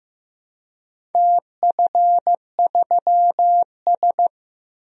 Morse code, heard as a keyed tone, about 15 wpm, 700 Hz. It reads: TF3S
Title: DXCC-100 Call Signs - 4 Characters - Repeated 2x - 15wpm